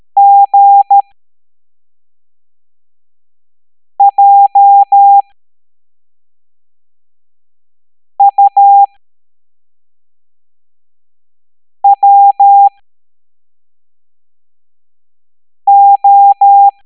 Morse code, heard as a keyed tone, about 13 wpm, spaced as for 3 wpm, 800 Hz. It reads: GJUWO